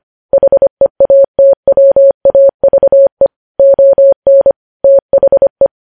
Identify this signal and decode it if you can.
Morse code, heard as a keyed tone, about 25 wpm, 550 Hz. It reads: HEATWAVE ON THE
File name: podcast / News headlines-1x-25x25